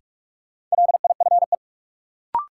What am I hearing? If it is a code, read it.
Morse code, heard as a keyed tone, about 45 wpm, 700 Hz. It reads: LIFE
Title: Sentences from Top 300 Words - New Word Drill - 45wpm